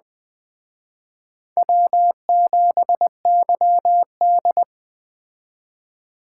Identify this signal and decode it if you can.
Morse code, heard as a keyed tone, about 21 wpm, 700 Hz. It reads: W7YD